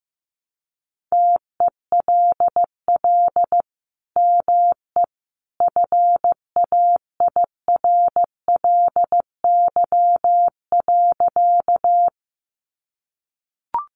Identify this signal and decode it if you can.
Morse code, heard as a keyed tone, about 15 wpm, 700 Hz. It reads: TELL ME FAIRLY.
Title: Sentences from Top 1100 Words 15wpm